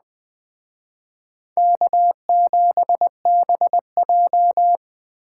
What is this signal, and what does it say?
Morse code, heard as a keyed tone, about 19 wpm, 700 Hz. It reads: K7BJ